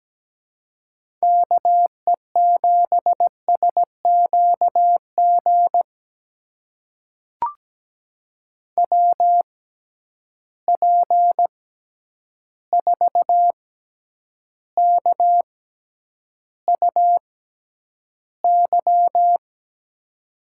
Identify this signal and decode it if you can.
Morse code, heard as a keyed tone, about 17 wpm, spaced as for 9 wpm, 700 Hz. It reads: KE7SQG WP4KUY
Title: USA Call Signs - 6 Characters - 6x Character Spacing - 17wpm